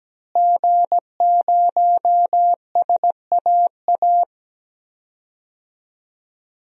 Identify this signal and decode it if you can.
Morse code, heard as a keyed tone, about 17 wpm, 700 Hz. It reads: G0SAA
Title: DXCC-100 Call Signs - 5 Characters - Repeated 2x - 17wpm